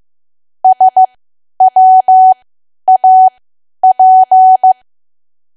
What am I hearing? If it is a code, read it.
Morse code, heard as a keyed tone, about 15 wpm, spaced as for 10 wpm, 750 Hz. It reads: SWAP
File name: cwa / W1AW-Level2 / 10WPM